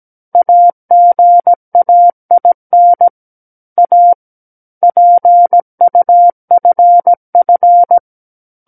Morse code, heard as a keyed tone, about 17 wpm, 700 Hz. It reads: AGAIN A PUFF
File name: Book One / 08-Chapter 8 - 17 WPM